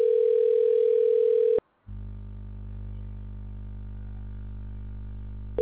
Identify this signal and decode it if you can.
Ringback tone (North America), 440 + 480 Hz.